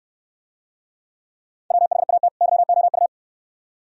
Morse code, heard as a keyed tone, about 64 wpm, 700 Hz. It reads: WHAT YOU